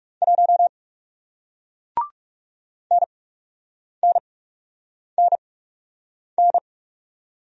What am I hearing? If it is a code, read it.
Morse code, heard as a keyed tone, about 40 wpm, spaced as for 8 wpm, 700 Hz. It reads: 1 NNNN